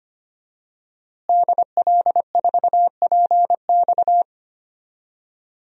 Morse code, heard as a keyed tone, about 25 wpm, 700 Hz. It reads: DL4PX
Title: DXCC-100 Call Signs - 5 Characters - Repeated 3x - 25wpm